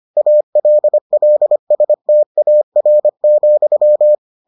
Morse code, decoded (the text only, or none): ALLSTAR,